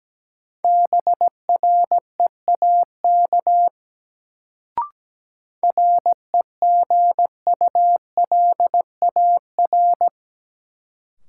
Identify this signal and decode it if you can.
Morse code, heard as a keyed tone, about 17 wpm, 700 Hz. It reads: BREAK REGULAR
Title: Sentences from Top 900 Words - New Word Drill - 17wpm